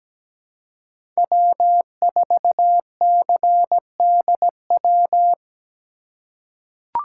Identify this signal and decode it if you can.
Morse code, heard as a keyed tone, about 17 wpm, 700 Hz. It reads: W4CDW
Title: USA Call Signs - 5 Characters - 3x Character Spacing - 17wpm